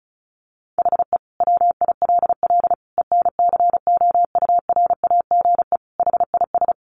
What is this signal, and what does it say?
Morse code, heard as a keyed tone, about 34 wpm, 700 Hz. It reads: HE WILL ENCOURAGE HIS
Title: Sentences from Top 1400 Words 35wpm